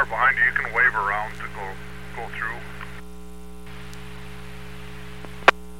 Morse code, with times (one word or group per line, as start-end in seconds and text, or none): none